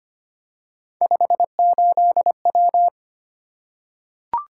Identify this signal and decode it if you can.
Morse code, heard as a keyed tone, about 25 wpm, 700 Hz. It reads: 58W
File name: Three Letter-Number Combinations - 25wpm